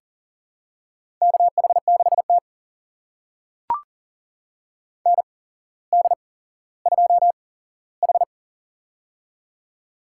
Morse code, heard as a keyed tone, about 38 wpm, spaced as for 17 wpm, 700 Hz. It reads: KH6T ND2H
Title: USA Call Signs - 4 Characters - 8x Character Spacing - 40wpm